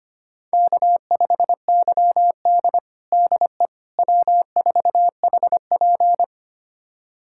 Morse code, heard as a keyed tone, about 25 wpm, 700 Hz. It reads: K5YD DE W4HP
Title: USA Callsign DE Callsign - 4 Characters - 25wpm